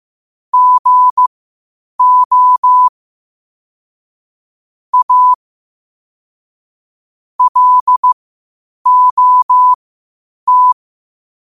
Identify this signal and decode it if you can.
Morse code, heard as a keyed tone, about 15 wpm, spaced as for 8 wpm, 1000 Hz. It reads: GO A LOT
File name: qotdfiles / 8wpm